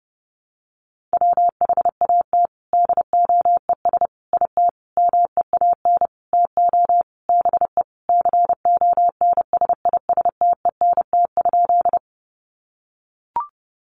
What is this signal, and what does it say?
Morse code, heard as a keyed tone, about 28 wpm, 700 Hz. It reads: WHAT DOES IT MEAN TO BE CONSISTENT?
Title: Sentences from Top 1500 Words 30wpm